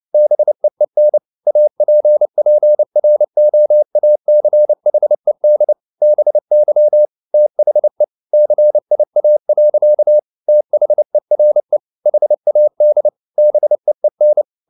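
Morse code, heard as a keyed tone, about 29 wpm, 600 Hz. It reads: BEEN APPROACHED BY THE CIA. THERE HAD BEEN
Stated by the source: Morse code at 29 wpm, sending Ken Follett's novel Never in English, from CW med Gnister.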